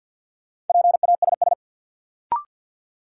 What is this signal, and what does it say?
Morse code, heard as a keyed tone, about 50 wpm, 700 Hz. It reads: PASS
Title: Sentences from Top 400 Words - New Word Drill - 50wpm